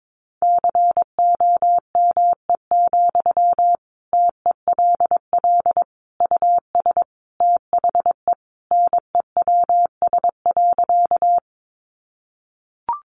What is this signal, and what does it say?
Morse code, heard as a keyed tone, about 22 wpm, 700 Hz. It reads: COME, TELL US THE NEWS.
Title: Sentences from Top 800 Words 22wpm